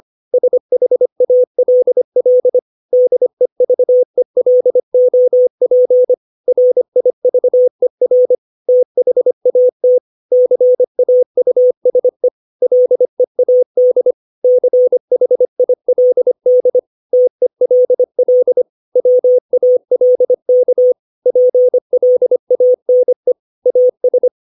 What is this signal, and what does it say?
Morse code, heard as a keyed tone, about 25 wpm, 500 Hz. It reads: SHALL DEVELOP RIVER THAT CAUSE LEAD CHILD TELL WALK PLANE AS
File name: 632 words / 632 Words 25WPM